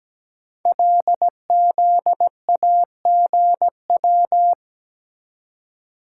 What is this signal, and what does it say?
Morse code, heard as a keyed tone, about 17 wpm, 700 Hz. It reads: LZAGW
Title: Code Groups - 5 Letter:Numbers - 17wpm